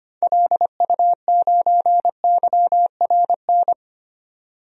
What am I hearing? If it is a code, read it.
Morse code, heard as a keyed tone, about 25 wpm, 700 Hz. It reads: LU9YRN
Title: DXCC-100 Call Signs - 6 Characters - 25wpm